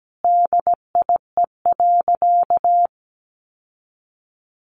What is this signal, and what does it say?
Morse code, heard as a keyed tone, about 17 wpm, 700 Hz. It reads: DIE.
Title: Sentences from Top 1300 Words 17wpm